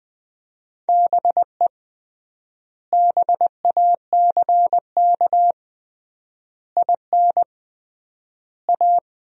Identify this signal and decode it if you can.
Morse code, heard as a keyed tone, about 20 wpm, 700 Hz. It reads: BE BACK IN A
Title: 3x Word-Spacing - Sentences from Top 500 Words - 20wpm